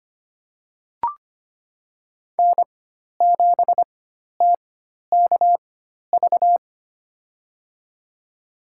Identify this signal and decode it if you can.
Morse code, heard as a keyed tone, about 25 wpm, spaced as for 12 wpm, 700 Hz. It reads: N7TKV
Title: USA Call Signs - 4x Character Spacing - 25wpm